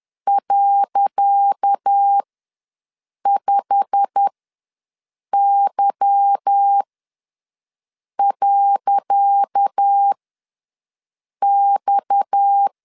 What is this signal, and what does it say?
Morse code, heard as a keyed tone, about 10 wpm, spaced as for 5 wpm, 800 Hz. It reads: .5Y.X
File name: LEZIONI / 10 SIGNS / 10C DOT ws 10wpm 4213